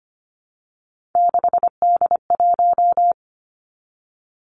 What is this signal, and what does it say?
Morse code, heard as a keyed tone, about 25 wpm, 700 Hz. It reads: 6D1
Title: Three Challenging Letter-Number Combinations 25wpm